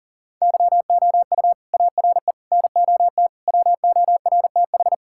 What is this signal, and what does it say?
Morse code, heard as a keyed tone, about 37 wpm, 700 Hz. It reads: YOU ARE NOT WORTH
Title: Sentences from Top 1600 Words 40wpm